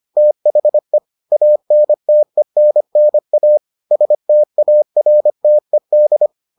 Morse code, heard as a keyed tone, about 25 wpm, 600 Hz. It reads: THE ANTENNA STARTED